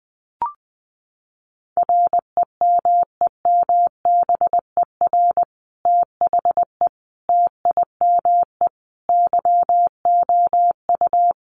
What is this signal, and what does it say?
Morse code, heard as a keyed tone, about 20 wpm, 700 Hz. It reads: REMEMBER THE TIME YOU
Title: Sentences from Top 1400 Words 20wpm